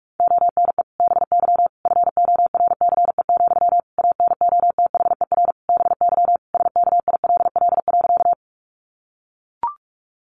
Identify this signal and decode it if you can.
Morse code, heard as a keyed tone, about 44 wpm, 700 Hz. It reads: ONE BY FORCE, ANOTHER BY SKILL.
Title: Sentences from Top 1200 Words 45wpm